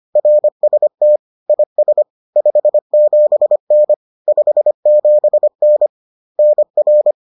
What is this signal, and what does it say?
Morse code, heard as a keyed tone, about 25 wpm, 600 Hz. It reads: RST IS 57N 57N NR